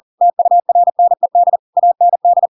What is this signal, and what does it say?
Morse code, heard as a keyed tone, about 40 wpm, 700 Hz. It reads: TURNED AND
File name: Pinocchio / Pinocchio ch4 40 40 ik1hge